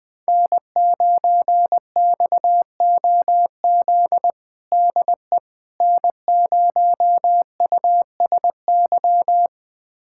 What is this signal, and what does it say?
Morse code, heard as a keyed tone, about 20 wpm, 700 Hz. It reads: N9XOZ DE N0USY
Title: USA Callsign DE Callsign - 5 Characters - 20wpm